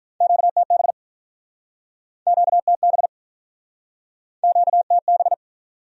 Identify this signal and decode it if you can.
Morse code, heard as a keyed tone, about 48 wpm, 700 Hz. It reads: QTB QTB QTB